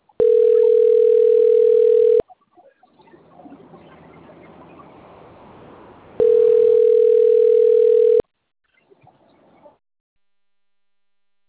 Ringback tone (North America), 440 + 480 Hz.
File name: rg-1600-unknown- - - .46398